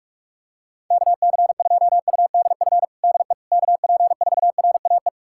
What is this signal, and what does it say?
Morse code, heard as a keyed tone, about 43 wpm, 700 Hz. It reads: KC2UDF DE KP4RAE
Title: USA Callsign DE Callsign - 6 Characters - 45wpm